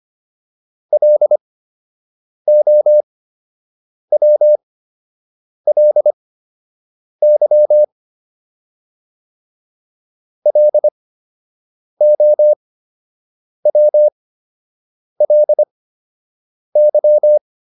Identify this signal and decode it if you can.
Morse code, heard as a keyed tone, about 25 wpm, spaced as for 7 wpm, 600 Hz. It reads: LOWLY LOWLY